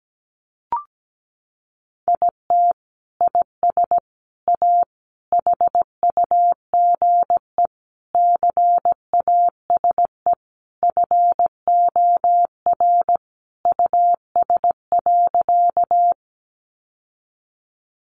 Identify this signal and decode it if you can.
Morse code, heard as a keyed tone, about 17 wpm, 700 Hz. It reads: IT IS A HUGE CASE FOR US.
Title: Sentences from Top 1100 Words 17wpm